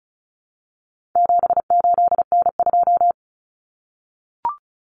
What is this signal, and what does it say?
Morse code, heard as a keyed tone, about 34 wpm, 700 Hz. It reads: 78N2
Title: Common Cut Numbers - 4-Digit 35wpm